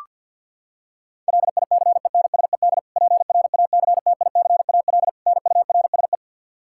Morse code, heard as a keyed tone, about 52 wpm, 700 Hz. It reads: LICENSED PRACTICAL NURSE
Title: Practice QSO - Exchange 3 - More Info - 50wpm